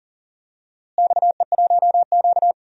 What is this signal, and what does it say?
Morse code, heard as a keyed tone, about 37 wpm, 700 Hz. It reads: XE1Q
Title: DXCC-100 Call Signs - 4 Characters - Repeated 2x - 40wpm